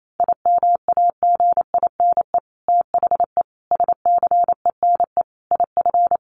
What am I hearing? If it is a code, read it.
Morse code, heard as a keyed tone, about 28 wpm, 700 Hz. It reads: IMAGINE THE SCENE IF
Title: Sentences from Top 1400 Words 28wpm